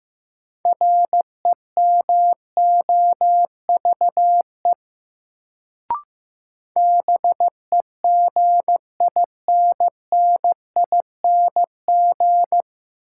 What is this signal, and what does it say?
Morse code, heard as a keyed tone, about 16 wpm, 700 Hz. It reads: REMOVE BEGINNING
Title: Sentences from Top 800 Words - New Word Drill - 15wpm